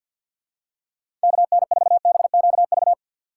Morse code, heard as a keyed tone, about 49 wpm, 700 Hz. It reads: KN4BQV
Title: USA Call Signs - 2x Character Spacing - 50wpm